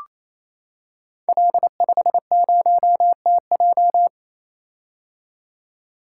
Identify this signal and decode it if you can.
Morse code, heard as a keyed tone, about 28 wpm, 700 Hz. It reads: L50TJ
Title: Code Groups - 5 Letter:Numbers - 28wpm